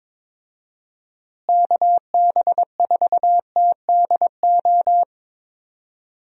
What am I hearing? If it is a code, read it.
Morse code, heard as a keyed tone, about 21 wpm, 700 Hz. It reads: KB4TDO
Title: USA Call Signs - 6 Characters - Repeated 3x - 22wpm